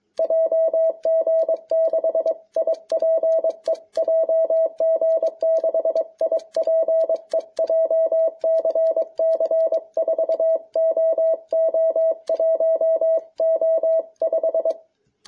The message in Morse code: JG6IPE JG6IPE JCC 4OO1O5

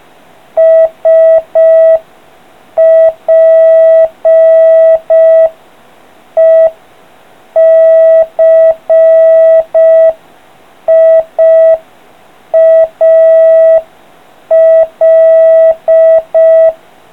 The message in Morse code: SPECIAL

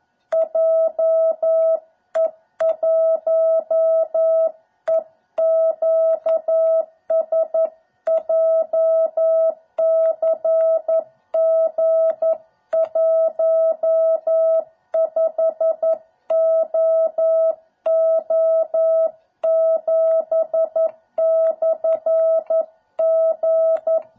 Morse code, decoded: JE1EQS JCG15OO7/G

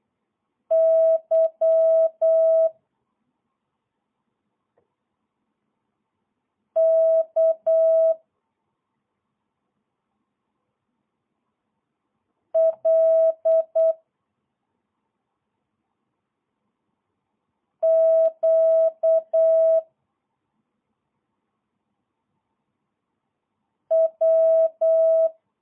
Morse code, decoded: Y K L Q W

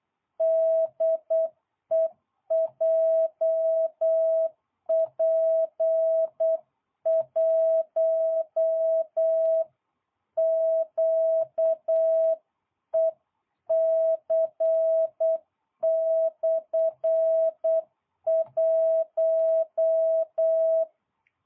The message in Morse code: DE JP1QEC/1

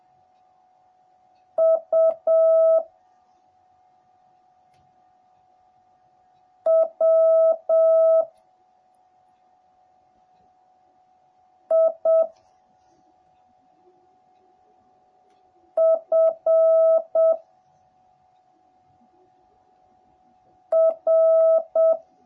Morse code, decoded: U W I F R

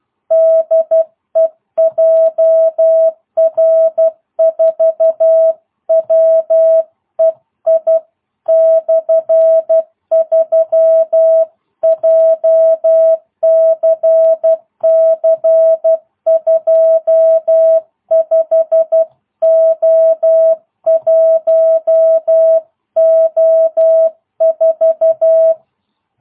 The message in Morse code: DE JR4WEI/3 JCC 25O1O4

